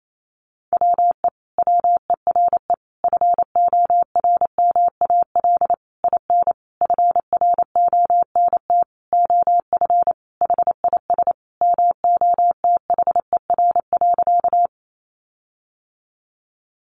Morse code, heard as a keyed tone, about 28 wpm, 700 Hz. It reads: WE WERE FORMAL IN FRONT OF HIS MOTHER.
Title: Sentences from Top 1300 Words 28wpm